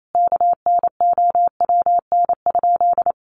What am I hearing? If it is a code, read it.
Morse code, heard as a keyed tone, about 28 wpm, 700 Hz. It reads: KNOWN?
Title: Sentences from Top 700 Words 28wpm